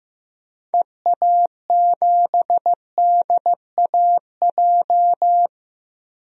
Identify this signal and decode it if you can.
Morse code, heard as a keyed tone, about 15 wpm, 700 Hz. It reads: EA7DAJ
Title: DXCC-100 Call Signs - 6 Characters - Repeated 2x - 15wpm